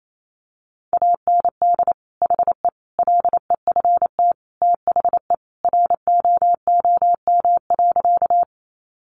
Morse code, heard as a keyed tone, about 28 wpm, 700 Hz. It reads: AND HE LEFT THE ROOM.
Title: Sentences from Top 300 Words 28wpm